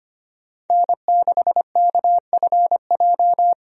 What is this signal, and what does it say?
Morse code, heard as a keyed tone, about 25 wpm, 690 Hz. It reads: N6KFJ
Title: USA Call Signs - 5 Characters - 25wpm